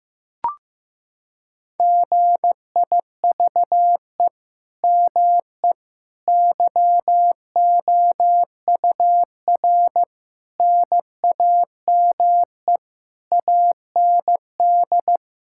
Morse code, heard as a keyed tone, about 16 wpm, 700 Hz. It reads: GIVE ME YOUR NAME AND
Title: Sentences from Top 1000 Words - Review Part 6 - 15wpm